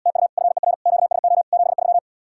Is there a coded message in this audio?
Yes: Morse code from 0.1 to 2.0 s, heard as a keyed tone, about 79 wpm, 700 Hz. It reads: TU FER QSO 73